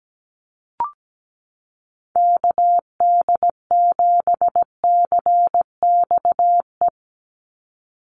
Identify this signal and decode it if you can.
Morse code, heard as a keyed tone, about 17 wpm, 700 Hz. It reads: KD7CXE
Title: USA Call Signs 17wpm